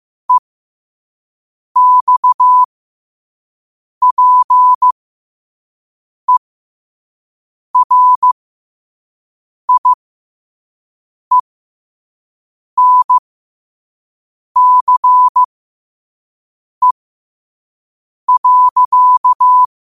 Morse code, heard as a keyed tone, about 15 wpm, spaced as for 5 wpm, 1000 Hz. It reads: EXPERIENCE.
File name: 5wpm